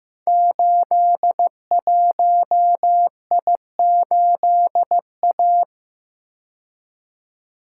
Morse code, heard as a keyed tone, about 15 wpm, 700 Hz. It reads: 81I8A
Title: Code Groups - 5 Letter:Numbers - 15wpm